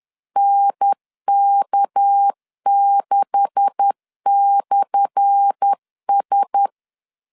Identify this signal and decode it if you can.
Morse code, heard as a keyed tone, about 10 wpm, 800 Hz. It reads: NK6/S
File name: LEZIONI / 10 SIGNS / 10A SLASH 10wpm 2808